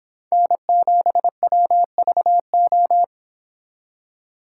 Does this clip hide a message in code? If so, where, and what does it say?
Yes: Morse code from 0.3 to 3.0 s, heard as a keyed tone, about 30 wpm, 700 Hz. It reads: N7WVO